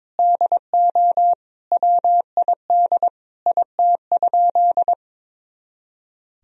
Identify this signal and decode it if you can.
Morse code, heard as a keyed tone, about 22 wpm, 700 Hz. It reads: DO WID IT?
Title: Sentences from Top 100 Words and Abbreviations 22wpm